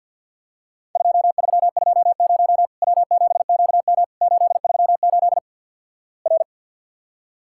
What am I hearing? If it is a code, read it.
Morse code, heard as a keyed tone, about 50 wpm, 700 Hz. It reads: 2320 W7QM 838 R
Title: Worked All Europe DX Contest - QTC Exchange - 50wpm